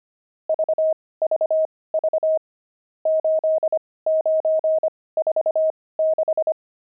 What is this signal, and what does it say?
Morse code, heard as a keyed tone, about 25 wpm, spaced as for 18 wpm, 630 Hz. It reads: VVV 8946
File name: pangram-fs-18-cs-25-nums2